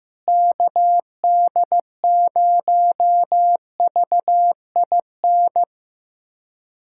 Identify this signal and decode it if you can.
Morse code, heard as a keyed tone, about 15 wpm, 700 Hz. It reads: KD0VIN
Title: USA Callsign DE Callsign 15wpm